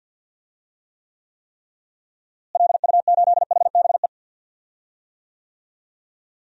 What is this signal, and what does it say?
Morse code, heard as a keyed tone, about 47 wpm, 700 Hz. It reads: LU8HBE
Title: DXCC-100 Call Signs - 6 Characters - Repeated 3x - 50wpm